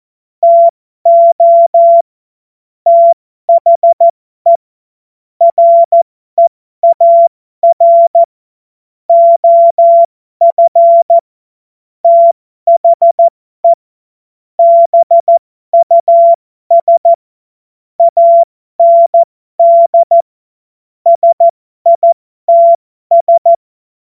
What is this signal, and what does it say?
Morse code, heard as a keyed tone, about 14 wpm, spaced as for 12 wpm, 680 Hz. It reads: TO THE REAR OF THE BUS AND SITS
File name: Morsekurs / Witze